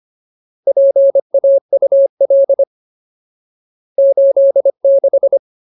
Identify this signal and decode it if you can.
Morse code, heard as a keyed tone, about 25 wpm, 550 Hz. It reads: PAUL 86
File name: Ultimate Word List / 25WPM Ultimate